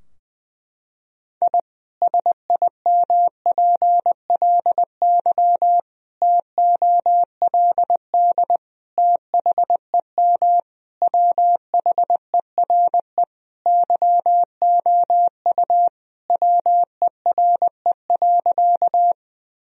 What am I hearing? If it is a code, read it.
Morse code, heard as a keyed tone, about 19 wpm, 700 Hz. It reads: I SIMPLY TOLD THEM WHERE YOU WERE.
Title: Sentences from Top 1000 Words - Review Part 4 - 20wpm